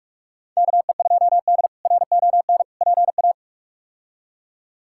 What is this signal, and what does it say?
Morse code, heard as a keyed tone, about 46 wpm, 700 Hz. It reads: KE2D RON PA
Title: K1USN SST QSO Practice - 45wpm